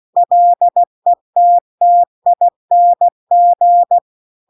Morse code, heard as a keyed tone, about 16 wpm, 700 Hz. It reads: LETTING
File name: CW-The-Big-Twoheaded-River-2-16WPM